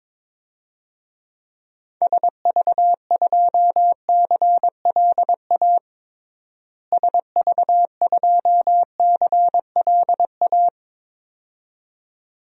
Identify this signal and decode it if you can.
Morse code, heard as a keyed tone, about 22 wpm, 700 Hz. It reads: SV2CLA SV2CLA
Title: DXCC-100 Call Signs - 6 Characters - Repeated 2x - 22wpm